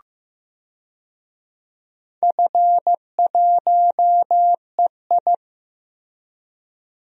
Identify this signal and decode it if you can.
Morse code, heard as a keyed tone, about 15 wpm, 700 Hz. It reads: F1EI
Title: DXCC-100 Call Signs - 4 Characters - Repeated 2x - 15wpm